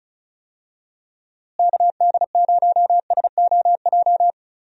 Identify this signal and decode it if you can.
Morse code, heard as a keyed tone, about 34 wpm, 700 Hz. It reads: KD0SOJ
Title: USA Call Signs - 6 Characters - Repeated 2x - 35wpm